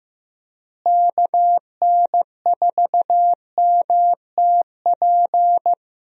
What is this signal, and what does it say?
Morse code, heard as a keyed tone, about 15 wpm, 700 Hz. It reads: KN4MTP
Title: USA Call Signs - 6 Characters - Repeated 3x - 15wpm